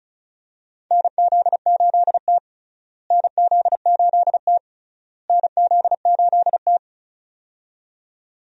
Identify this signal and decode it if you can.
Morse code, heard as a keyed tone, about 34 wpm, 700 Hz. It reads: NZ8T NZ8T NZ8T
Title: USA Call Signs - 4 Characters - Repeated 3x - 35wpm